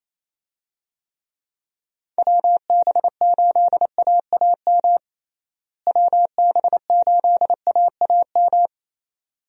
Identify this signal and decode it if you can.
Morse code, heard as a keyed tone, about 28 wpm, 700 Hz. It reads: WB8AAM WB8AAM
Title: USA Call Signs - 6 Characters - Repeated 2x - 28wpm